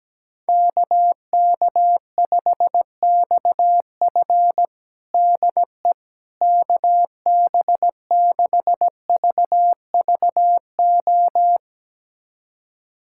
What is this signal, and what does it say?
Morse code, heard as a keyed tone, about 17 wpm, 700 Hz. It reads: KK5XF DE KB6VVO
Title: USA Callsign DE Callsign 17wpm